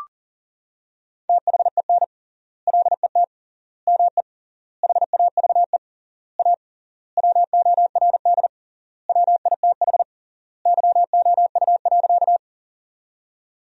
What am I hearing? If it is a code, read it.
Morse code, heard as a keyed tone, about 40 wpm, 700 Hz. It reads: THEN LET ME HAVE A WORD WITH YOU.
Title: 3x Word-Spacing - Sentences from Top 300 Words - 40wpm